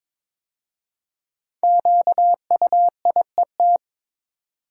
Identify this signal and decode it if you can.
Morse code, heard as a keyed tone, about 23 wpm, 700 Hz. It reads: QUIET
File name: 5 Letter Words - First 3 Letters Given - 22wpm